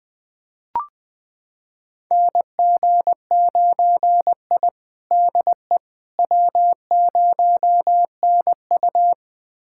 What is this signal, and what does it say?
Morse code, heard as a keyed tone, about 20 wpm, 700 Hz. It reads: NG9I DE W0NU
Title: USA Callsign DE Callsign - 4 Characters - 20wpm